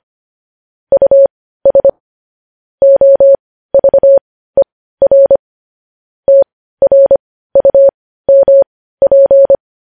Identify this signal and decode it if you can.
Morse code, heard as a keyed tone, about 25 wpm, spaced as for 15 wpm, 550 Hz. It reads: US OVER TRUMP
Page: news headlines in morse code at 15 wpm